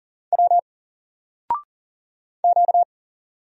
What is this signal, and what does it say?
Morse code, heard as a keyed tone, about 39 wpm, 700 Hz. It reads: W Q
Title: Single Letter-Number 40wpm